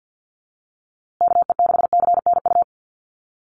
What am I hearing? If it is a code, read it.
Morse code, heard as a keyed tone, about 50 wpm, 700 Hz. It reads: KE6CNU
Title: USA Call Signs Repeated 2x 50wpm